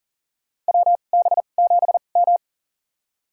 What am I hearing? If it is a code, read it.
Morse code, heard as a keyed tone, about 40 wpm, spaced as for 29 wpm, 700 Hz. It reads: WB7M
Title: USA Call Signs - 4 Characters - 2x Character Spacing - 40wpm